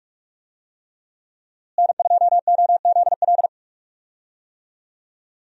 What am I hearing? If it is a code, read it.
Morse code, heard as a keyed tone, about 46 wpm, 700 Hz. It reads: N2OZL
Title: USA Call Signs - 5 Characters - Repeated 3x - 45wpm